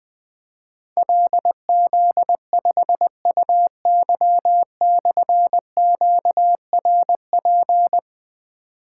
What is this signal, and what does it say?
Morse code, heard as a keyed tone, about 21 wpm, 700 Hz. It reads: LZ5UY/QRP
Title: Difficult Call Signs 20wpm